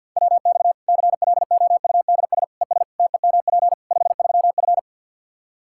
Morse code, heard as a keyed tone, about 53 wpm, 700 Hz. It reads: WX CLOUDS ES TEMP 53F
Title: Practice QSO - Exchange 2 - Info - 50wpm